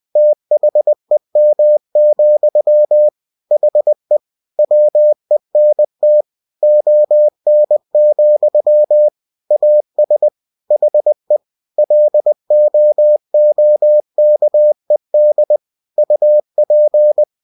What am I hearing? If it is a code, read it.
Morse code, heard as a keyed tone, about 20 wpm, 600 Hz. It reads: THEM, HE WENT ON, AS HE LOOKED UP
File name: cw-books / winnie-the-pooh / winnie-20wpm-page-0084